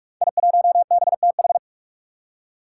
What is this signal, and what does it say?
Morse code, heard as a keyed tone, about 44 wpm, 700 Hz. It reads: I1BTH
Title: Five Challenging Letter-Number Combinations 45wpm